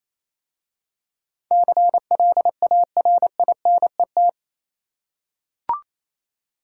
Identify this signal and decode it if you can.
Morse code, heard as a keyed tone, about 27 wpm, 700 Hz. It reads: CLARINET
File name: Easy Words - Speed-Racing with Hints - 22wpm